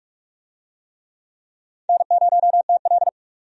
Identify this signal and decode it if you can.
Morse code, heard as a keyed tone, about 44 wpm, 700 Hz. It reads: N0TL
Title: USA Call Signs - 4 Characters - 45wpm